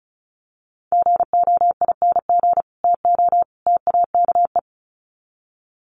Morse code, heard as a keyed tone, about 34 wpm, 700 Hz. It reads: GOING TO TAKE
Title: Sets of 3 Words 35wpm